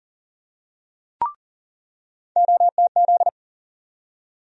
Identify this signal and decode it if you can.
Morse code, heard as a keyed tone, about 40 wpm, 700 Hz. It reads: OTZ
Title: USA Call Sign Postfixes - 3 Letters 40wpm